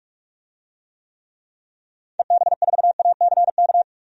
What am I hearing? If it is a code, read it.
Morse code, heard as a keyed tone, about 46 wpm, 700 Hz. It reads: EB4ACK
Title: DXCC-100 Call Signs - 6 Characters - 45wpm